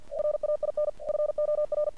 Morse code, heard as a keyed tone, about 49 wpm, 600 Hz. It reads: RAIN FOR
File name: archived / CRSnet